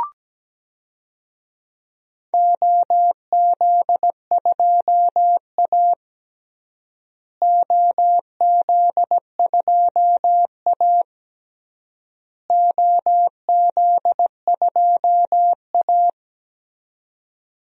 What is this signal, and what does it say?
Morse code, heard as a keyed tone, about 17 wpm, 700 Hz. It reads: OZ2A OZ2A OZ2A
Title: DXCC-100 Call Signs - 4 Characters - Repeated 3x - 17wpm